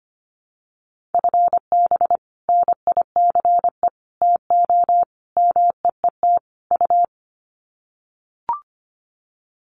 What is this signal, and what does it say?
Morse code, heard as a keyed tone, about 25 wpm, 700 Hz. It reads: FB NICE TO MEET U
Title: Common QSO Phrases 25wpm